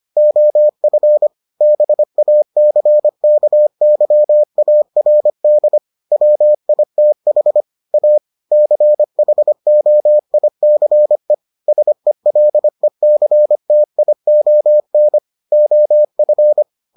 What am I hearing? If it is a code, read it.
Morse code, heard as a keyed tone, about 25 wpm, 600 Hz. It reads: OF BACKYARD WITH A CHOICE SELECTION OF